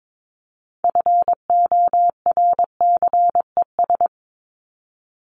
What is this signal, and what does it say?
Morse code, heard as a keyed tone, about 22 wpm, 700 Hz. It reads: FORCES
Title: ES:S Suffix - Top 100 Words with Base Words 22wpm